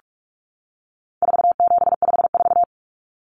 Morse code, heard as a keyed tone, about 44 wpm, 700 Hz. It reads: 4754
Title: Common Cut Numbers - 4-Digit 45wpm